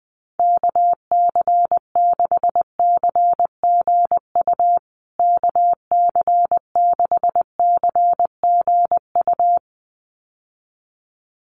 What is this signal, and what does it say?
Morse code, heard as a keyed tone, about 20 wpm, 700 Hz. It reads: KC6CGU KC6CGU
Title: USA Call Signs Repeated 2x 20wpm